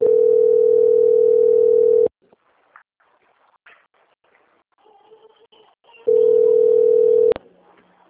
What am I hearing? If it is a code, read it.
Ringback tone (North America), 440 + 480 Hz.
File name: out- - - - - .9965